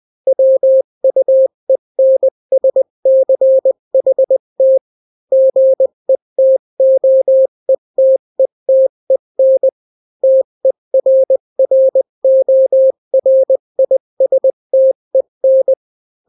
Morse code, heard as a keyed tone, about 20 wpm, 530 Hz. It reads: WUENSCHT GETOETETEN TERRORISTEN
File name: daily / 18wpm